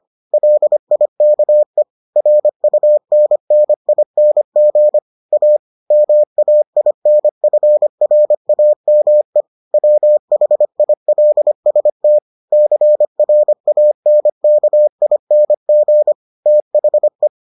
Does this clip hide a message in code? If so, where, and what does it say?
Yes: Morse code from 0.3 to 17.3 s, heard as a keyed tone, about 25 wpm, 600 Hz. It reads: LIKE RUNNING A MAINFRAME WHILST CRANKING THE